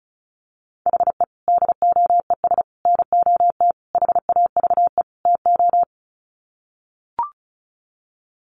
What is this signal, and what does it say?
Morse code, heard as a keyed tone, about 34 wpm, 700 Hz. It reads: HE DOES NOT HAVE TO
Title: Sets of 5 Words Encore 35wpm